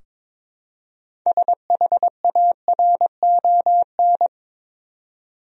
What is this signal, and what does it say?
Morse code, heard as a keyed tone, about 22 wpm, 700 Hz. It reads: SHARON
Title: USA Common Names - 22wpm